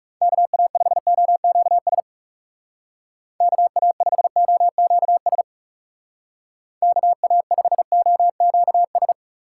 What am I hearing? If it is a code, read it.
Morse code, heard as a keyed tone, about 36 wpm, 700 Hz. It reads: KA5OQS KA5OQS KA5OQS
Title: Speed Racing - USA 2x3 Call Signs 30wpm